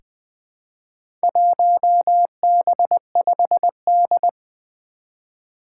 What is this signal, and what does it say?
Morse code, heard as a keyed tone, about 20 wpm, 700 Hz. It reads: 1B5D
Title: Four Challenging Letter-Number Combinations 20wpm